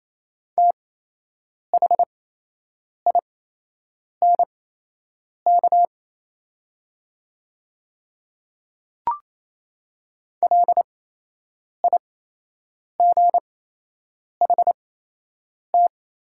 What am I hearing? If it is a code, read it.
Morse code, heard as a keyed tone, about 28 wpm, spaced as for 8 wpm, 700 Hz. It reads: THINK LIGHT
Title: 8x Character Spacing - Top 300 Words - 28wpm